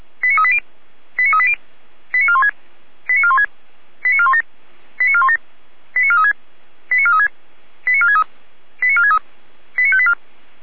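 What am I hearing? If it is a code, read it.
Five-tone selective call (ZVEI), heard as quick runs of five tones, tones of about 70 ms. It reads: 89380 89380 89416 89416 89417 89417 89436 89436 89463 89463 89584
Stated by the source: Received by SDR.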